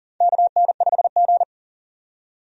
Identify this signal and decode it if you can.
Morse code, heard as a keyed tone, about 40 wpm, 700 Hz. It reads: KN5G
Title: USA Call Signs - 4 Characters - 40wpm